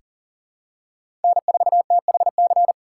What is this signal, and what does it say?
Morse code, heard as a keyed tone, about 37 wpm, 700 Hz. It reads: N4THC
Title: USA Call Signs - 5 Characters - 4x Character Spacing - 40wpm